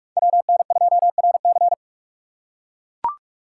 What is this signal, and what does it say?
Morse code, heard as a keyed tone, about 44 wpm, 700 Hz. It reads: WN2RC